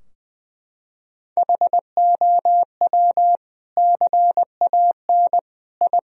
Morse code, heard as a keyed tone, about 20 wpm, 700 Hz. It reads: HOW CAN I
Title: Sentences from Top 1000 Words - Review Part 2 - 20wpm